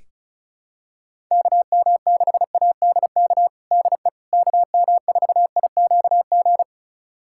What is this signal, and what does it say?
Morse code, heard as a keyed tone, about 36 wpm, 700 Hz. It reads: KM6ADK DE KM4IQG
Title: USA Callsign DE Callsign - 6 Characters - 35wpm